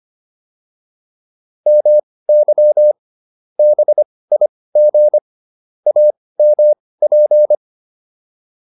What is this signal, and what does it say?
Morse code, heard as a keyed tone, about 25 wpm, spaced as for 18 wpm, 600 Hz. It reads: MY BIG AMP